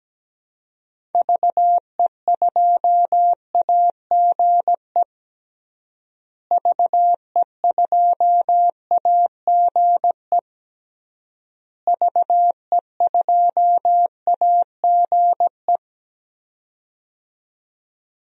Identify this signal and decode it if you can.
Morse code, heard as a keyed tone, about 17 wpm, 700 Hz. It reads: VE2AGE VE2AGE VE2AGE
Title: DXCC-100 Call Signs - 6 Characters - Repeated 3x - 17wpm